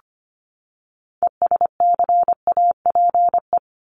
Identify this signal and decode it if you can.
Morse code, heard as a keyed tone, about 25 wpm, 700 Hz. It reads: ESCAPE
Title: Top 1501-1750 Words 25wpm